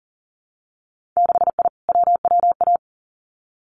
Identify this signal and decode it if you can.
Morse code, heard as a keyed tone, about 42 wpm, 700 Hz. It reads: 6I WWA
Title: Winter Field Day Exchanges Encore 40wpm